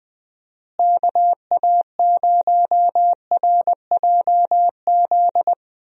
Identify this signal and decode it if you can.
Morse code, heard as a keyed tone, about 21 wpm, 700 Hz. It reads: KA0RJZ
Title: USA Call Signs - 6 Characters - Repeated 3x - 20wpm